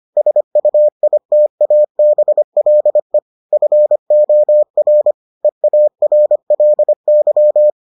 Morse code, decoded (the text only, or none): SUITABLE FOR EARLY